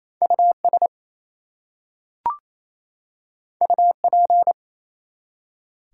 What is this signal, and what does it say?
Morse code, heard as a keyed tone, about 27 wpm, 700 Hz. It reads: US UP
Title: Two Letter Words - 28wpm